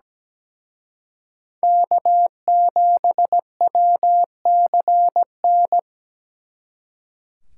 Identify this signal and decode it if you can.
Morse code, heard as a keyed tone, about 20 wpm, 700 Hz. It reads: K7WCN
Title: Speed Racing - USA 1x3 Call Signs 17wpm